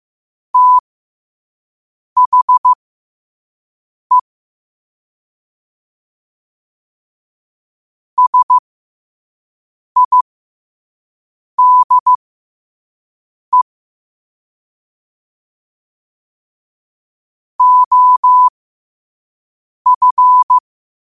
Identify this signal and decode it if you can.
Morse code, heard as a keyed tone, about 15 wpm, spaced as for 5 wpm, 1000 Hz. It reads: THE SIDE OF
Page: New quotes every day in morse code at 5 Words per minute.